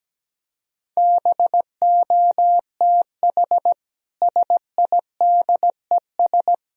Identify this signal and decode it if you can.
Morse code, heard as a keyed tone, about 17 wpm, 700 Hz. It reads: BOTH SIDES
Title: Speed-Racing - Sets of 4 Words 17wpm